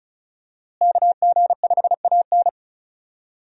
Morse code, heard as a keyed tone, about 36 wpm, 700 Hz. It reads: KG5AN
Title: USA Call Signs - 5 Characters - Repeated 2x - 35wpm